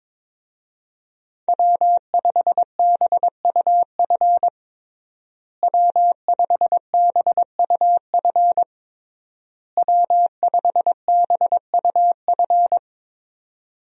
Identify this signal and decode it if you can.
Morse code, heard as a keyed tone, about 21 wpm, 700 Hz. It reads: W5BUF W5BUF W5BUF
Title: USA Call Signs - 5 Characters - Repeated 3x - 22wpm